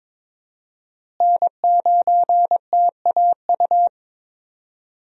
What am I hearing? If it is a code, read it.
Morse code, heard as a keyed tone, about 22 wpm, 700 Hz. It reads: N9TAU
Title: USA Call Signs - 5 Characters - Repeated 2x - 22wpm